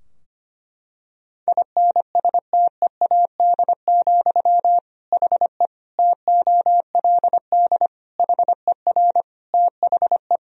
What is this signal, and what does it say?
Morse code, heard as a keyed tone, about 25 wpm, 700 Hz. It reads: INSTEAD, HE TOLD HER THE
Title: Sentences from Top 1000 Words - Review Part 4 - 25wpm